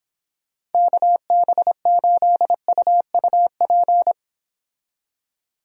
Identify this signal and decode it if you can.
Morse code, heard as a keyed tone, about 30 wpm, 700 Hz. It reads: KB8UUP